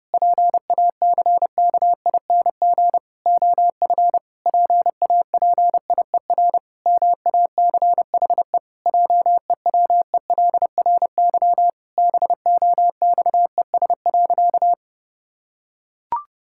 Morse code, heard as a keyed tone, about 30 wpm, 700 Hz. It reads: PACKING OF PAPIER MACHE JEWELRY BOXES.
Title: Pangram Challenge v2 - 30wpm